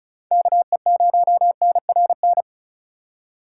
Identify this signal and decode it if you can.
Morse code, heard as a keyed tone, about 34 wpm, 700 Hz. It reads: KE0NRN
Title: USA Call Signs - 6 Characters - 6x Character Spacing - 35wpm